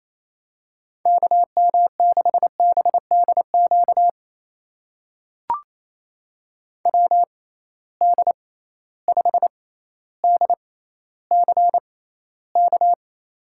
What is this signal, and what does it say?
Morse code, heard as a keyed tone, about 28 wpm, spaced as for 14 wpm, 700 Hz. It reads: KM6BDQ WD5DCK
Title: USA Call Signs - 6 Characters - 6x Character Spacing - 28wpm